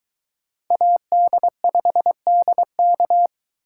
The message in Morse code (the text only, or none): AD5DK